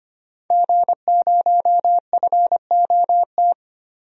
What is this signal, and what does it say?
Morse code, heard as a keyed tone, about 25 wpm, 700 Hz. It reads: G0FOT
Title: DXCC-100 Call Signs - 5 Characters - Repeated 3x - 25wpm